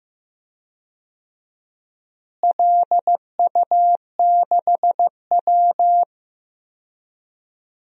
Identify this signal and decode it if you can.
Morse code, heard as a keyed tone, about 15 wpm, 700 Hz. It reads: LU6W
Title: DXCC-100 Call Signs - 4 Characters - Repeated 3x - 15wpm